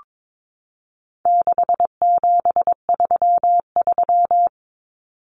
Morse code, heard as a keyed tone, about 22 wpm, 700 Hz. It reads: 6733
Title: Numbers - 4-Digit 22wpm